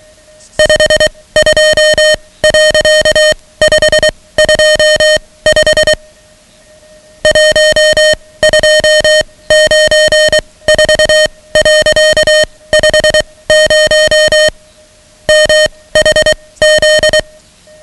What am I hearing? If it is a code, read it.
Morse code, heard as a keyed tone, about 23 wpm, spaced as for 17 wpm, 620 Hz, with noise. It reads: 52.525 1294.50 MHZ
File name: ham radio / CodeQuick / Code Quick 10